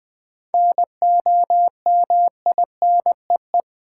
Morse code, heard as a keyed tone, about 20 wpm, 700 Hz. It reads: NOMINEE
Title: 7 Letter Words - First 3 Letters Given - 20wpm